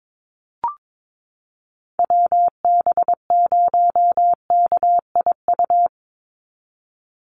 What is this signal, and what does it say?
Morse code, heard as a keyed tone, about 22 wpm, 700 Hz. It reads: WB0KIU
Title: NA SOTA Chaser Callsigns 22wpm